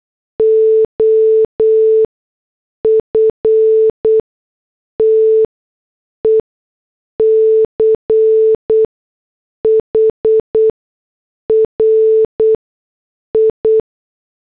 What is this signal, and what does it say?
Morse code, heard as a keyed tone, about 8 wpm, spaced as for 6 wpm, 440 Hz. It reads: OF TECHRI